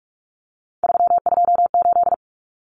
Morse code, heard as a keyed tone, about 44 wpm, 700 Hz. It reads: 328
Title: Common Cut Numbers - 3-Digit 45wpm